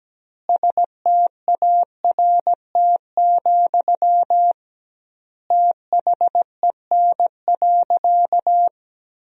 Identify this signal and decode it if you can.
Morse code, heard as a keyed tone, about 17 wpm, 700 Hz. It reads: START, THEN.
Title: 2x Word-Spacing - Sentences from Top 300 Words - 17wpm